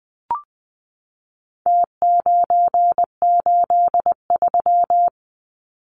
Morse code, heard as a keyed tone, about 20 wpm, 700 Hz. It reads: T983
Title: Numbers - Contest Style - 4-Digit 20wpm